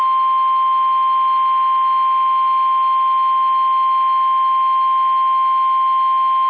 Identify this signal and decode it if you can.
Emergency Alert System (SAME) tones (the 1050 Hz attention tone). It is heard from 0.2 to 5.6 s.
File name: eas / KUSO / KUSO